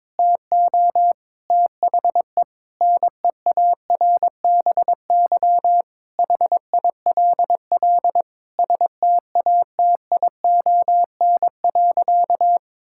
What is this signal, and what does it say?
Morse code, heard as a keyed tone, about 22 wpm, 700 Hz. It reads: TO THE NEARBY HILL STATION.
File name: Difficult Sentences with Hints - 22wpm